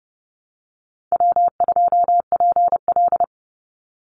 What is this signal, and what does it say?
Morse code, heard as a keyed tone, about 30 wpm, 700 Hz. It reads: W2PL
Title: Easy Call Signs - 30wpm